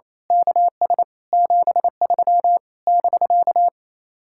Morse code, heard as a keyed tone, about 26 wpm, 700 Hz. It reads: KS 73 <BK>
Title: Speed-Racing Common QSO Phrases - 28wpm